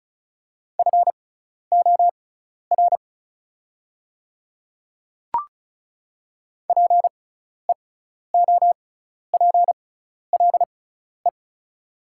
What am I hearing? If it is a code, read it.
Morse code, heard as a keyed tone, about 36 wpm, spaced as for 12 wpm, 700 Hz. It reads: FOR PEOPLE
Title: 6x Character Spacing - Top 200 Words - 35wpm